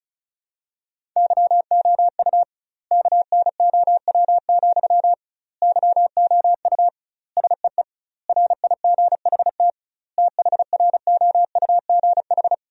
Morse code, heard as a keyed tone, about 36 wpm, 700 Hz. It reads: YOU KNOW, YOU SEE RIGHT THROUGH
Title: 2x Word-Spacing - Sentences from Top 200 Words - 35wpm